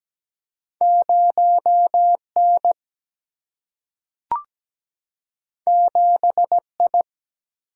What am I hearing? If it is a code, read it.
Morse code, heard as a keyed tone, about 17 wpm, 700 Hz. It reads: 0 N 7 I
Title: Single-Letter Single-Number 17wpm